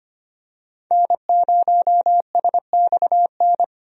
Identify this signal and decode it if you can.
Morse code, heard as a keyed tone, about 25 wpm, 700 Hz. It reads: N0SXN
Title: USA Call Signs - 5 Characters - Repeated 2x - 25wpm